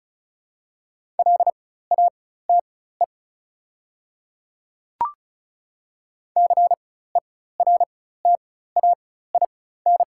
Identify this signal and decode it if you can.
Morse code, heard as a keyed tone, about 36 wpm, spaced as for 16 wpm, 700 Hz. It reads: LATE CERTAIN
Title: 4x Character Spacing - Top 500 Words - 35wpm